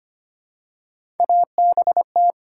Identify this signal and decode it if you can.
Morse code, heard as a keyed tone, about 25 wpm, 700 Hz. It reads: ABT
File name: Top 90 QSO Elements - 6x Character Spacing - 25wpm